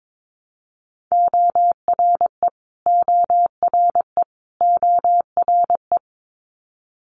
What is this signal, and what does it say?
Morse code, heard as a keyed tone, about 22 wpm, 700 Hz. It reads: ORE ORE ORE
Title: Multi-letter Phonemes Repeated 3x 22wpm